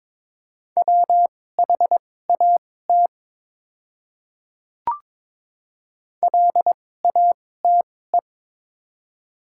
Morse code, heard as a keyed tone, about 22 wpm, spaced as for 16 wpm, 700 Hz. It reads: WHAT LATE
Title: 2x Character Spacing - Top 500 Words - 22wpm